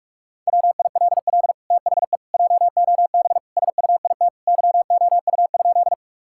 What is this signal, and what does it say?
Morse code, heard as a keyed tone, about 46 wpm, 700 Hz. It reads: WILL THE JOB SUIT YOU?
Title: Sentences from Top 1600 Words 45wpm